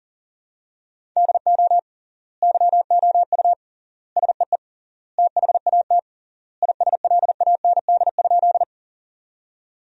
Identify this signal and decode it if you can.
Morse code, heard as a keyed tone, about 40 wpm, 700 Hz. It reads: DO YOU SEE THAT ISLAND?
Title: 3x Word-Spacing - Sentences from Top 500 Words - 40wpm